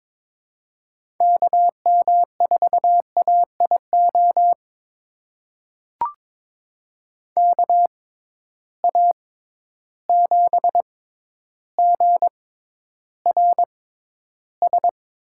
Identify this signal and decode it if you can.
Morse code, heard as a keyed tone, about 22 wpm, spaced as for 11 wpm, 700 Hz. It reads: KM4AIO KA7GRS
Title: USA Call Signs - 6 Characters - 6x Character Spacing - 22wpm